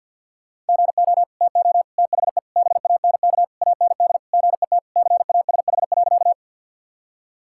Morse code, heard as a keyed tone, about 52 wpm, 700 Hz. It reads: GO TO THE BANK AND GET CASH.
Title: Sentences from Top 1000 Words - Review Part 5 - 50wpm